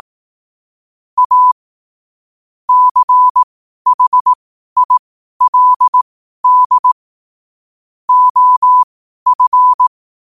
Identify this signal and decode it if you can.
Morse code, heard as a keyed tone, about 18 wpm, spaced as for 12 wpm, 1000 Hz. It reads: A CHILD OF
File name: qotdfiles / 12wpm